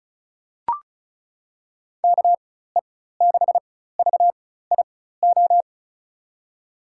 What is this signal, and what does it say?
Morse code, heard as a keyed tone, about 36 wpm, spaced as for 16 wpm, 700 Hz. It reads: KE6VIO
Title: USA Call Signs - 6 Characters - 4x Character Spacing - 35wpm